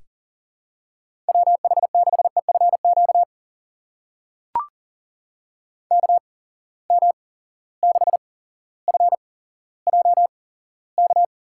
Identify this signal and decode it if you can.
Morse code, heard as a keyed tone, about 38 wpm, spaced as for 17 wpm, 700 Hz. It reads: WH6EFQ KM6FJK
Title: USA Call Signs - 6 Characters - 8x Character Spacing - 40wpm